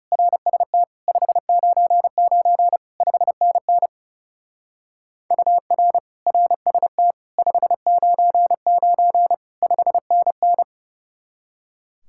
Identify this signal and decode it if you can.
Morse code, heard as a keyed tone, about 36 wpm, 700 Hz. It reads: RST 599 5NN UR RST 599 5NN
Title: Speed-Racing Common QSO Phrases - 30wpm